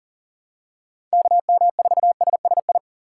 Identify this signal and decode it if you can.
Morse code, heard as a keyed tone, about 40 wpm, 700 Hz. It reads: KM4SSI
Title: USA Call Signs - 6 Characters - 4x Character Spacing - 40wpm